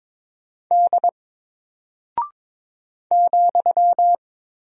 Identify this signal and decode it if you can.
Morse code, heard as a keyed tone, about 22 wpm, 700 Hz. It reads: D ,
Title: Single Character (Letter:Number:Punctuation:Prosign) Alt 22wpm